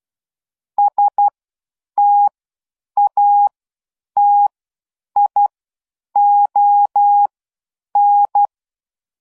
Morse code, heard as a keyed tone, about 12 wpm, spaced as for 8 wpm, 820 Hz. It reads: STATION